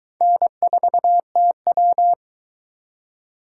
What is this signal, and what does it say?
Morse code, heard as a keyed tone, about 20 wpm, 700 Hz. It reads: N4TW